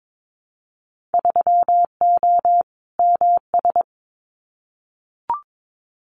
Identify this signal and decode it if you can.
Morse code, heard as a keyed tone, about 22 wpm, 700 Hz. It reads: 3O MS
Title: Winter Field Day Exchanges Encore 22wpm